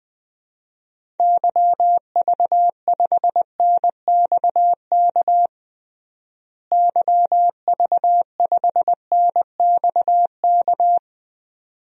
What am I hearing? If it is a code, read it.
Morse code, heard as a keyed tone, about 20 wpm, 700 Hz. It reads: YV5NXK YV5NXK
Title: DXCC-100 Call Signs - 6 Characters - Repeated 2x - 20wpm